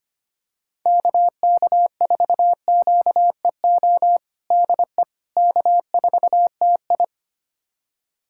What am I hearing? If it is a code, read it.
Morse code, heard as a keyed tone, about 25 wpm, 700 Hz. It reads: KK4QEO DE K4TI
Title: USA Callsign DE Callsign 25wpm